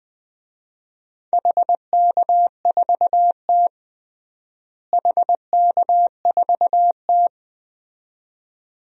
Morse code, heard as a keyed tone, about 20 wpm, 700 Hz. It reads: HK4T HK4T
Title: DXCC-100 Call Signs - 4 Characters - Repeated 2x - 20wpm